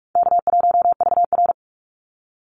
Morse code, heard as a keyed tone, about 44 wpm, 700 Hz. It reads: K1VR
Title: Call Signs Repeated 2x - 45wpm